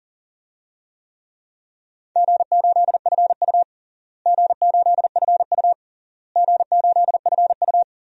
Morse code, heard as a keyed tone, about 40 wpm, 700 Hz. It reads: G8FU G8FU G8FU
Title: DXCC-100 Call Signs - 4 Characters - Repeated 3x - 40wpm